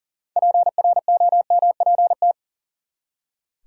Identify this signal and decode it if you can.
Morse code, heard as a keyed tone, about 40 wpm, 700 Hz. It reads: PROMPT
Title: Six Letter Words - 40wpm